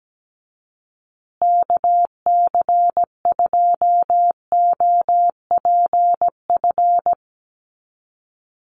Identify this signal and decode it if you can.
Morse code, heard as a keyed tone, about 17 wpm, 700 Hz. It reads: KC2OPF
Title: USA Call Signs Repeated 2x 17wpm